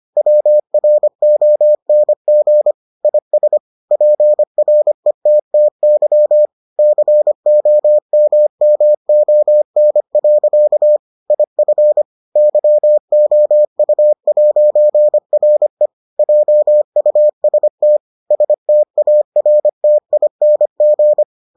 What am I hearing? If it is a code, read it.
Morse code, heard as a keyed tone, about 25 wpm, 600 Hz. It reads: WRONG IS PRETTY COMMON. IF YOU'RE JUST STARTING